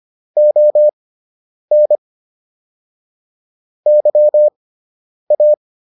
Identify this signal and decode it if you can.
Morse code, heard as a keyed tone, about 25 wpm, spaced as for 9 wpm, 600 Hz. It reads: ON YA